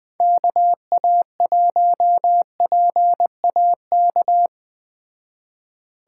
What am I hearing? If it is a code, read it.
Morse code, heard as a keyed tone, about 21 wpm, 700 Hz. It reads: KA1PAK
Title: Speed Racing - USA 2x3 Call Signs 17wpm